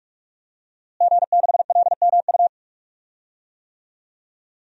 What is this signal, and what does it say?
Morse code, heard as a keyed tone, about 46 wpm, 700 Hz. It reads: G6LMU
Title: DXCC-100 Call Signs - 5 Characters - Repeated 3x - 45wpm